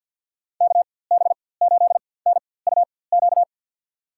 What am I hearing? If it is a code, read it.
Morse code, heard as a keyed tone, about 52 wpm, spaced as for 23 wpm, 700 Hz. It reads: KB8NUQ
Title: USA Call Signs - 6 Characters - 4x Character Spacing - 50wpm